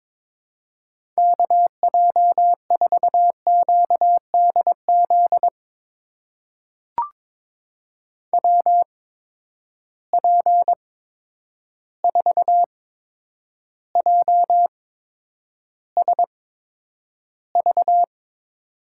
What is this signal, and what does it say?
Morse code, heard as a keyed tone, about 22 wpm, spaced as for 9 wpm, 700 Hz. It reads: KJ4QDZ WP4JSV
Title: USA Call Signs - 6 Characters - 8x Character Spacing - 22wpm